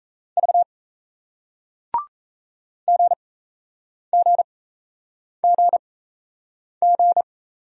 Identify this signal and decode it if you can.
Morse code, heard as a keyed tone, about 38 wpm, spaced as for 8 wpm, 700 Hz. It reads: V GGGG